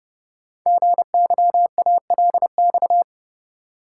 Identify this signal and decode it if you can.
Morse code, heard as a keyed tone, about 30 wpm, 700 Hz. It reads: GYALX
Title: Code Groups - 5 Letters - 30wpm